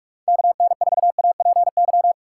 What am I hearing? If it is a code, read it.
Morse code, heard as a keyed tone, about 42 wpm, 700 Hz. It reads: KN4APY
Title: USA Call Signs - 6 Characters - 3x Character Spacing - 45wpm